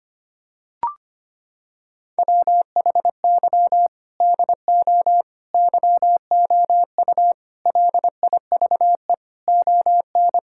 Morse code, heard as a keyed tone, about 25 wpm, 700 Hz. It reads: WHY DO YOU LIVE ON
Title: Sentences from Top 1000 Words - Review Part 6 - 25wpm